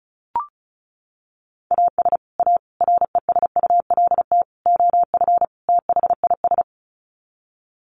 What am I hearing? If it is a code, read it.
Morse code, heard as a keyed tone, about 34 wpm, 700 Hz. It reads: AS A RESULT OF THIS
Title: Sets of 5 Words 35wpm